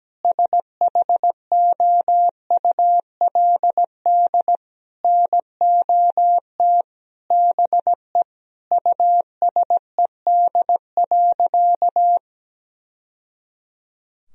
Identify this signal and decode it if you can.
Morse code, heard as a keyed tone, about 17 wpm, 700 Hz. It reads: SHOULD NOT BE USED.
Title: Sentences from Top 1000 Words - Review Part 4 - 17wpm